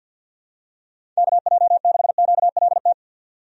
Morse code, heard as a keyed tone, about 52 wpm, 700 Hz. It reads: KJ6QLT
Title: USA Call Signs - 6 Characters - 50wpm